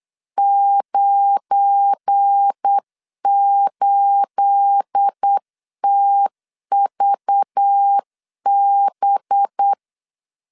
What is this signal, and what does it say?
Morse code, heard as a keyed tone, about 8 wpm, 800 Hz. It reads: 98TVB
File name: LEZIONI / 9 28 8wpm 3512